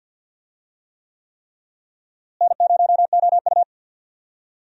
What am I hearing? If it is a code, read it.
Morse code, heard as a keyed tone, about 50 wpm, 700 Hz. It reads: N0OU
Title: USA Call Signs - 4 Characters - Repeated 3x - 50wpm